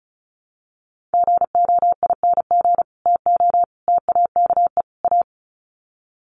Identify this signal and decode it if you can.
Morse code, heard as a keyed tone, about 34 wpm, 700 Hz. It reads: GOING TO TAKE A
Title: Sets of 4 Words 35wpm